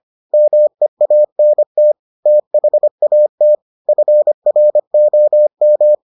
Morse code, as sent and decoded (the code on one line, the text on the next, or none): -- . .- -. - / - .... .- - / ..-. .-. --- --
MEANT THAT FROM